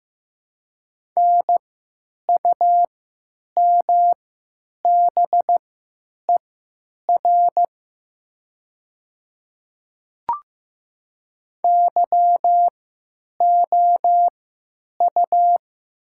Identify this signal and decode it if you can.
Morse code, heard as a keyed tone, about 15 wpm, spaced as for 9 wpm, 700 Hz. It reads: NUMBER YOU
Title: 3x Character Spacing - Top 100 Words - 15wpm